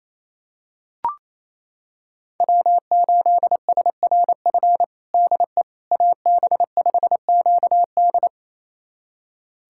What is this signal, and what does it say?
Morse code, heard as a keyed tone, about 27 wpm, 700 Hz. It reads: W8SRF DE AB5QD
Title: USA Callsign DE Callsign - 5 Characters - 28wpm